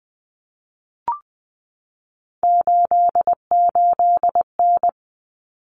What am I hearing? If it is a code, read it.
Morse code, heard as a keyed tone, about 20 wpm, 700 Hz. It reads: 88N
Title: Common Cut Numbers - 3-Digit 20wpm